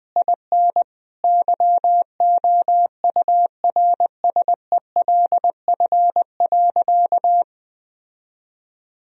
Morse code, heard as a keyed tone, about 20 wpm, 700 Hz. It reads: IN YOURSELF.
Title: Sentences from Top 1000 Words - Review Part 1 - 20wpm